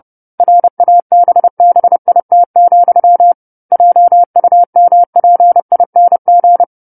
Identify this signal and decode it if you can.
Morse code, heard as a keyed tone, about 30 wpm, 700 Hz. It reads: RABBIT, JUMPING
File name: 10-Chapter-10 30 WPM